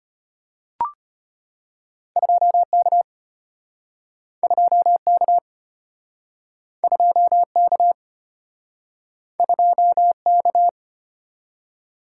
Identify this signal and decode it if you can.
Morse code, heard as a keyed tone, about 34 wpm, 700 Hz. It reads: 2K 2K 2K 2K